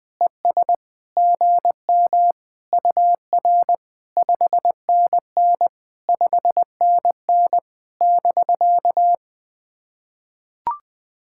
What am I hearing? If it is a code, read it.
Morse code, heard as a keyed tone, about 20 wpm, 700 Hz. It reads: ES GM UR 5NN 5NN <BK>
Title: POTA QSO Practice - 20wpm